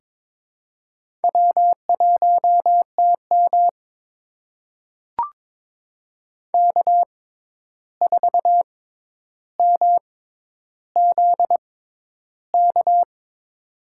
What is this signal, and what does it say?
Morse code, heard as a keyed tone, about 22 wpm, spaced as for 11 wpm, 700 Hz. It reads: W1TM K4MZK
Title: USA Call Signs - 6x Character Spacing - 22wpm